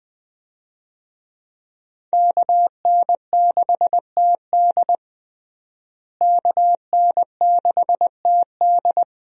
Morse code, heard as a keyed tone, about 21 wpm, 700 Hz. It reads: KN6TD KN6TD
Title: USA Call Signs - 5 Characters - Repeated 2x - 20wpm